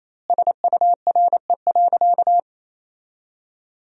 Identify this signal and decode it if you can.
Morse code, heard as a keyed tone, about 28 wpm, 700 Hz. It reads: SURE.
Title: 4x Word-Spacing - Sentences from Top 500 Words - 28wpm